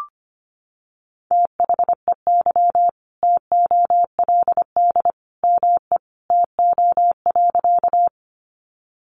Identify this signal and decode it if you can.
Morse code, heard as a keyed tone, about 25 wpm, 700 Hz. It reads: THEY TOLD ME TO.
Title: Sentences from Top 400 Words 25wpm